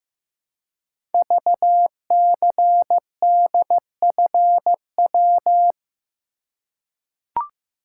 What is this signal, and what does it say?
Morse code, heard as a keyed tone, about 15 wpm, 700 Hz. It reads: VCDFW
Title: Code Groups - 5 Letters - 15wpm